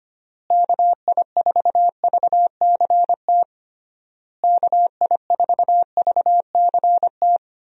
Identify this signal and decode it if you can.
Morse code, heard as a keyed tone, about 25 wpm, 700 Hz. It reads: KI4VCT KI4VCT
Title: USA Call Signs - 6 Characters - Repeated 3x - 25wpm